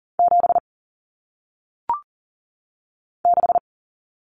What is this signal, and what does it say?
Morse code, heard as a keyed tone, about 38 wpm, 700 Hz. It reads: 7 6
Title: Common Cut Numbers - 1-Digit 40wpm